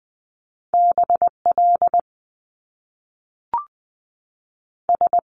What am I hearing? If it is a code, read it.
Morse code, heard as a keyed tone, about 20 wpm, 700 Hz. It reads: B L S
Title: Top Two Letter Combinations in Words 20wpm